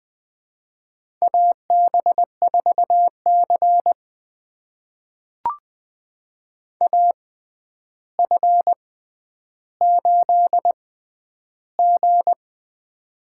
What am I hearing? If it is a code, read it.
Morse code, heard as a keyed tone, about 21 wpm, spaced as for 10 wpm, 700 Hz. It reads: AB4C AF8G
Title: USA Call Signs - 4 Characters - 6x Character Spacing - 20wpm